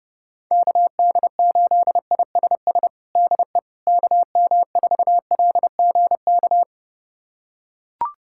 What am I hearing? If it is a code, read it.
Morse code, heard as a keyed tone, about 30 wpm, 690 Hz. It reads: KD8ISS DE KM4LGK
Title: USA Callsign DE Callsign - 6 Characters - 30wpm